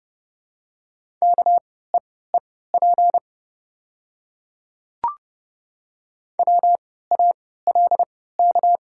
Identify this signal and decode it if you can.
Morse code, heard as a keyed tone, about 30 wpm, spaced as for 17 wpm, 700 Hz. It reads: KEEP WALK
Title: 3x Character Spacing - Top 500 Words - 30wpm